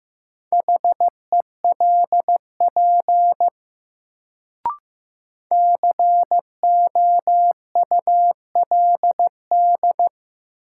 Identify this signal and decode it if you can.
Morse code, heard as a keyed tone, about 15 wpm, 700 Hz. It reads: HELP COULD
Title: Sets of 2 Words - New Word Drill - 15wpm